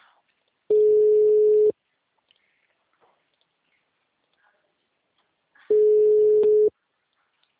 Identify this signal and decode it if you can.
Ringback tone (Europe, 425 Hz).